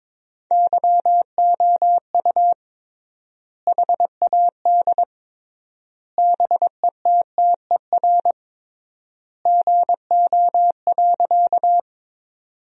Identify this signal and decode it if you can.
Morse code, heard as a keyed tone, about 22 wpm, 700 Hz. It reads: YOU HAD BETTER GO.
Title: 3x Word-Spacing - Sentences from Top 400 Words - 22wpm